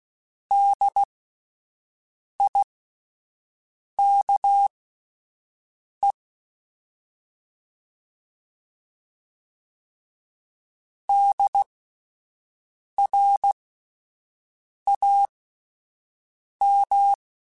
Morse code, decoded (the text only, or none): DIKE DRAM